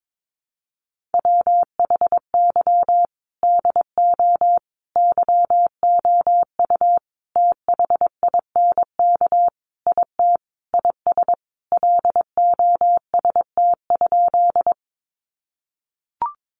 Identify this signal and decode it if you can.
Morse code, heard as a keyed tone, about 22 wpm, 700 Hz. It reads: WHY DO YOU THINK IT IS LOST?
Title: Sentences from Top 1400 Words 22wpm